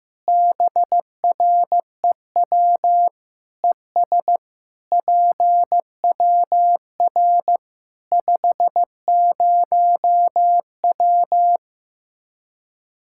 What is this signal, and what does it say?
Morse code, heard as a keyed tone, about 15 wpm, 700 Hz. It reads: BREW ES PWR 50W
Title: Practice QSO - Exchange 2 - Info - 15wpm